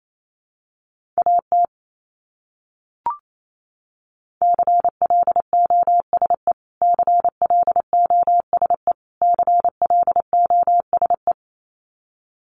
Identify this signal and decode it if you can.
Morse code, heard as a keyed tone, about 28 wpm, 700 Hz. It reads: AT CLOSE CLOSE CLOSE
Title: Top 500 Words Repeated 28wpm